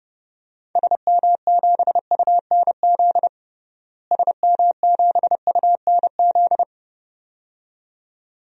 Morse code, heard as a keyed tone, about 30 wpm, 700 Hz. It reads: SM7UNZ SM7UNZ
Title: DXCC-100 Call Signs - 6 Characters - Repeated 2x - 30wpm